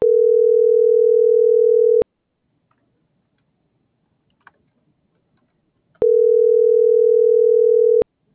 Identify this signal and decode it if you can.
Ringback tone (North America), 440 + 480 Hz.